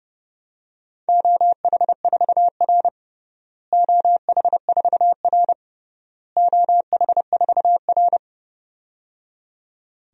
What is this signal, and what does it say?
Morse code, heard as a keyed tone, about 30 wpm, 700 Hz. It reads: OH4R OH4R OH4R
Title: DXCC-100 Call Signs - 4 Characters - Repeated 3x - 30wpm